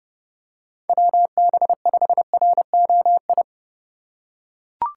Morse code, heard as a keyed tone, about 30 wpm, 700 Hz. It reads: WB5ROI